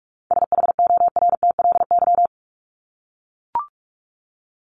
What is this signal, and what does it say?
Morse code, heard as a keyed tone, about 44 wpm, 700 Hz. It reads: SHORTLY
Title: LY Suffix - Top 100 Words with Base Words 45wpm